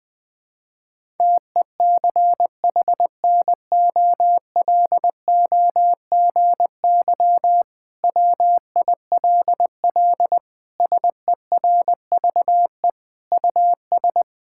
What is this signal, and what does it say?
Morse code, heard as a keyed tone, about 21 wpm, 700 Hz. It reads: TECHNOLOGY WILL SERVE US
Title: Sentences from Top 1000 Words - Review Part 5 - 20wpm